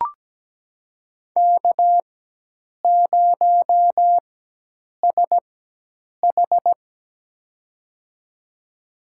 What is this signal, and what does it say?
Morse code, heard as a keyed tone, about 17 wpm, spaced as for 8 wpm, 700 Hz. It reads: K0SH
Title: USA Call Signs - 4 Characters - 4x Character Spacing - 17wpm